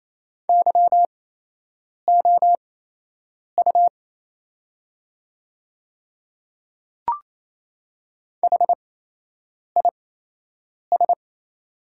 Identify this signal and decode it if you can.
Morse code, heard as a keyed tone, about 27 wpm, spaced as for 8 wpm, 690 Hz. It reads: YOU HIS
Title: 8x Character Spacing - Top 100 Words - 28wpm